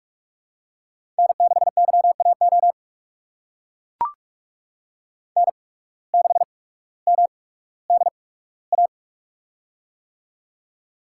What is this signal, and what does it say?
Morse code, heard as a keyed tone, about 42 wpm, spaced as for 19 wpm, 700 Hz. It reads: N6YAO N6MDA
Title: USA Call Signs - 5 Characters - 8x Character Spacing - 45wpm